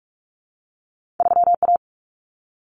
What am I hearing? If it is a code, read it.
Morse code, heard as a keyed tone, about 46 wpm, 700 Hz. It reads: 3A
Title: Common Cut Numbers - 2-Digit 45wpm